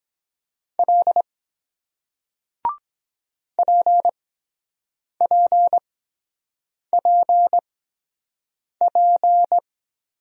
Morse code, heard as a keyed tone, about 23 wpm, spaced as for 7 wpm, 700 Hz. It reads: L PPPP